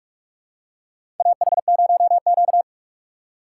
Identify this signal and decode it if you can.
Morse code, heard as a keyed tone, about 44 wpm, 700 Hz. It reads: AH0Q
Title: USA Call Signs - 4x Character Spacing - 45wpm